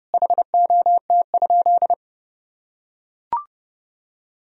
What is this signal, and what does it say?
Morse code, heard as a keyed tone, about 32 wpm, 700 Hz. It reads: HOT?
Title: Sentences from Top 1000 Words - Review Part 4 - 30wpm